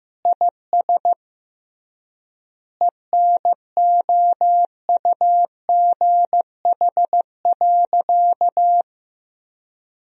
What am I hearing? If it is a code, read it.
Morse code, heard as a keyed tone, about 15 wpm, 700 Hz. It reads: IS ENOUGH.
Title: 3x Word-Spacing - Sentences from Top 400 Words - 15wpm